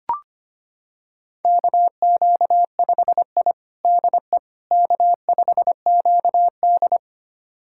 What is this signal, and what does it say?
Morse code, heard as a keyed tone, about 25 wpm, 700 Hz. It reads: KQ5I DE K5QD
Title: USA Callsign DE Callsign - 4 Characters - 25wpm